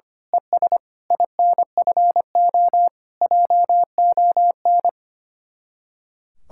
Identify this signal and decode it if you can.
Morse code, heard as a keyed tone, about 24 wpm, 700 Hz. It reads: ES INFO JON
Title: Speed-Racing Common QSO Phrases - 25wpm